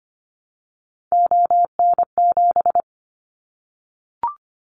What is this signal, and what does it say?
Morse code, heard as a keyed tone, about 25 wpm, 700 Hz. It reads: ON7
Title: Callsign Prefixes for Top 100 DXCC - 25wpm